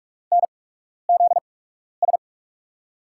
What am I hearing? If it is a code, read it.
Morse code, heard as a keyed tone, about 46 wpm, spaced as for 12 wpm, 700 Hz. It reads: NZS
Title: USA Call Signs - 5 Characters - 8x Character Spacing - 45wpm